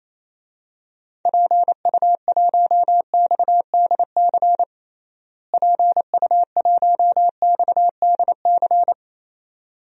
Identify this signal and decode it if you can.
Morse code, heard as a keyed tone, about 27 wpm, 700 Hz. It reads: PU1XDC PU1XDC
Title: DXCC-100 Call Signs - 6 Characters - Repeated 2x - 28wpm